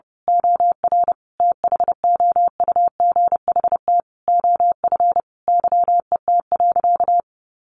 Morse code, heard as a keyed tone, about 30 wpm, 700 Hz. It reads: OR THOUGHT OF YET.
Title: A Sound Of Thunder 30wpm